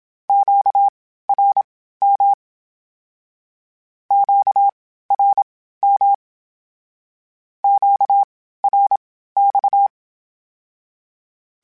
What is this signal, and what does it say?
Morse code, heard as a keyed tone, about 27 wpm, spaced as for 15 wpm, 800 Hz. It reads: QRM QRM QRX
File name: (spaced) / Q-Calls Spaced-mp3s / Q Calls Twice sp @25